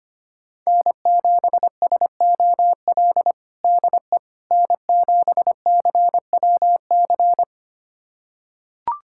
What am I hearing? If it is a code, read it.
Morse code, heard as a keyed tone, about 25 wpm, 700 Hz. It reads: N7SOL DE N7CWC
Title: USA Callsign DE Callsign - 5 Characters - 25wpm